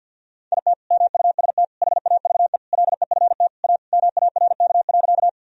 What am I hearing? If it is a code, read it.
Morse code, heard as a keyed tone, about 45 wpm, 700 Hz. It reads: IT MUST HAVE LEFT A MARK.
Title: Sentences from Top 1000 Words - Review Part 1 - 50wpm